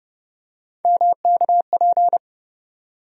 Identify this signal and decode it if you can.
Morse code, heard as a keyed tone, about 30 wpm, 700 Hz. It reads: MKP
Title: USA Call Sign Postfixes - 3 Letters 30wpm